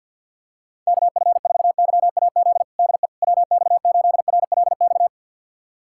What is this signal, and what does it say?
Morse code, heard as a keyed tone, about 52 wpm, 700 Hz. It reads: KF4YAZ DE WX8RLX
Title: USA Callsign DE Callsign 50wpm